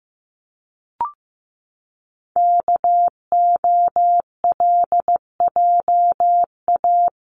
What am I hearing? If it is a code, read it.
Morse code, heard as a keyed tone, about 15 wpm, 700 Hz. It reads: KOLJA
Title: CWT Exchange 15wpm